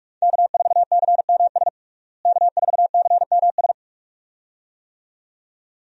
Morse code, heard as a keyed tone, about 45 wpm, 700 Hz. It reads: K4CMS K4CMS
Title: USA Call Signs - 5 Characters - Repeated 2x - 45wpm